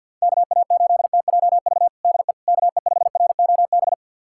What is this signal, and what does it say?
Morse code, heard as a keyed tone, about 52 wpm, 700 Hz. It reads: KA8TJV DE KE5ROB
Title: USA Callsign DE Callsign 50wpm